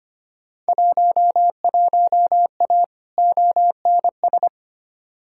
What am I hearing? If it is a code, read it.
Morse code, heard as a keyed tone, about 25 wpm, 700 Hz. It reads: 11A ONS
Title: Field Day Exchanges Encore - 25wpm